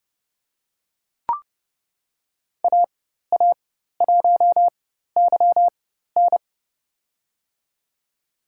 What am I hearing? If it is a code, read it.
Morse code, heard as a keyed tone, about 32 wpm, spaced as for 14 wpm, 700 Hz. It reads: AA1YN
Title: USA Call Signs - 5 Characters - 4x Character Spacing - 30wpm